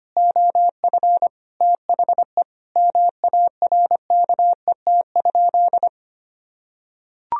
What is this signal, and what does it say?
Morse code, heard as a keyed tone, about 25 wpm, 700 Hz. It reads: OF THE MARKET?
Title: Sentences from Top 1000 Words - Review Part 3 - 25wpm